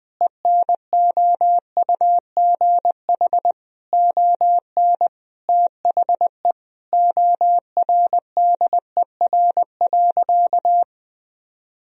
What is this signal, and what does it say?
Morse code, heard as a keyed tone, about 20 wpm, 700 Hz. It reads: ENOUGH ON THE ORDER.
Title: Sentences from Top 1000 Words - Review Part 2 - 20wpm